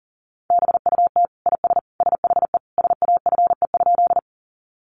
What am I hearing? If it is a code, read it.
Morse code, heard as a keyed tone, about 40 wpm, 700 Hz. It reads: BUT IS SHE SAFE?
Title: Sentences from Top 900 Words 40wpm